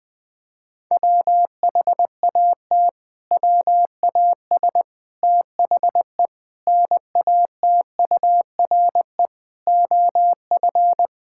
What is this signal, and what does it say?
Morse code, heard as a keyed tone, about 20 wpm, 700 Hz. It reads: WHAT WAS THE NATURE OF
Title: Sentences from Top 1000 Words - Review Part 6 - 20wpm